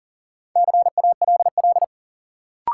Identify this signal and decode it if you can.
Morse code, heard as a keyed tone, about 40 wpm, 700 Hz. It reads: CALL
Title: Sentences from Top 100 Words - New Word Drill - 40wpm